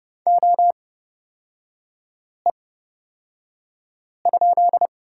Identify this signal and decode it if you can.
Morse code, heard as a keyed tone, about 30 wpm, spaced as for 5 wpm, 690 Hz. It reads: OE?
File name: Lesson 38e - Letter q - Three Characters 30wpm